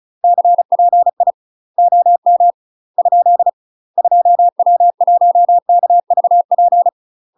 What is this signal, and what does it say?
Morse code, heard as a keyed tone, about 36 wpm, 700 Hz. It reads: CPI OM ? 2W1KVP
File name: UK MASTER QSOs TRAINING / 35WPM / UK MASTER QSOs 0031